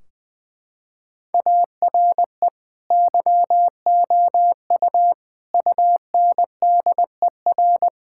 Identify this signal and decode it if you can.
Morse code, heard as a keyed tone, about 20 wpm, 700 Hz. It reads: ARE YOU UNDER
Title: Sentences from Top 1600 Words 20wpm